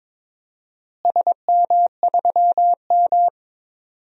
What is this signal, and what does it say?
Morse code, heard as a keyed tone, about 21 wpm, 700 Hz. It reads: SM3M
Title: DXCC-100 Call Signs - 4 Characters - Repeated 2x - 22wpm